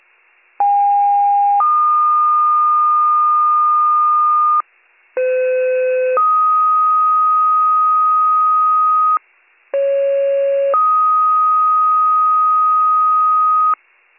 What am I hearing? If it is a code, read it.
Two-tone page, heard as a one-second tone, then a three-second tone, about 810 then 1220 Hz, about 510 then 1250 Hz, about 550 then 1250 Hz. It is heard from 0.6 to 13.7 s.